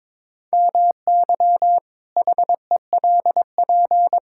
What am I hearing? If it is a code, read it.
Morse code, heard as a keyed tone, about 22 wpm, 700 Hz. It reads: MY HELP
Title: Pangram Challenge v2 - 22wpm